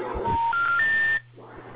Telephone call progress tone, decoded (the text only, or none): special information tone